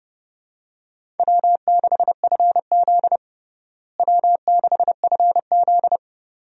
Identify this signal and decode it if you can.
Morse code, heard as a keyed tone, about 30 wpm, 700 Hz. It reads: W6FZ W6FZ
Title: USA Call Signs - 4 Characters - Repeated 2x - 30wpm